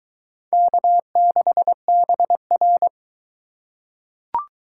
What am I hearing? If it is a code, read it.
Morse code, heard as a keyed tone, about 22 wpm, 700 Hz. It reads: K6BR